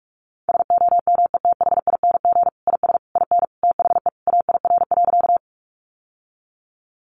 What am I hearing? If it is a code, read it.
Morse code, heard as a keyed tone, about 44 wpm, 700 Hz. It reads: SOMETHING IS IN THE AIR.
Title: Sentences from Top 600 Words 45wpm